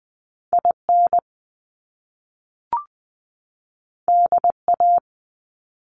Morse code, heard as a keyed tone, about 20 wpm, 700 Hz. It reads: IN DA
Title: Top Two Letter Combinations in Words 20wpm